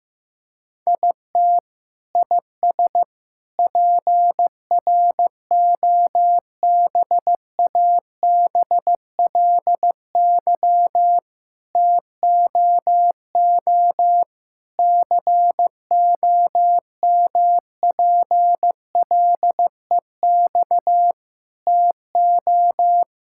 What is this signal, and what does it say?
Morse code, heard as a keyed tone, about 15 wpm, 700 Hz. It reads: IT IS PROBABLY TOO COMPLEX TO
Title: Sentences from Top 1000 Words - Review Part 6 - 15wpm